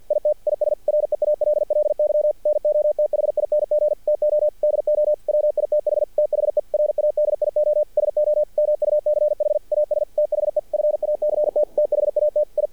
Morse code, heard as a keyed tone, about 50 wpm, 600 Hz. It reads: IT IS REALLY NOTHING TO DO WITH THE RADIO SO MUCH AS THE FACT THAT I
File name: archived / CRSnet